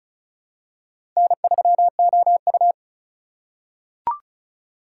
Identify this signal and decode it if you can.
Morse code, heard as a keyed tone, about 36 wpm, 700 Hz. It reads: N3OU
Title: USA Call Signs - 4 Characters - 4x Character Spacing - 35wpm